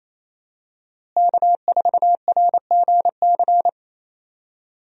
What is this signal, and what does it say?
Morse code, heard as a keyed tone, about 28 wpm, 700 Hz. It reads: K4RGC
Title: USA Call Signs - 5 Characters - Repeated 2x - 28wpm